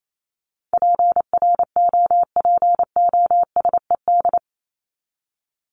Morse code, heard as a keyed tone, about 28 wpm, 700 Hz. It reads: PROPOSED
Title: ED Suffix - Top 100 Words with Base Words 28wpm